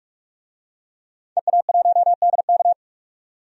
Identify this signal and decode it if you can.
Morse code, heard as a keyed tone, about 47 wpm, 700 Hz. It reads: EA1DK
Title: DXCC-100 Call Signs - 5 Characters - 45wpm